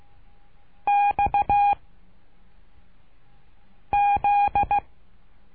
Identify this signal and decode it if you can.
Morse code, heard as a keyed tone, about 16 wpm, 820 Hz. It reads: X Z
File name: cursos / CURSO DE TELEGRAFIA / Telegrafia - Trilha 12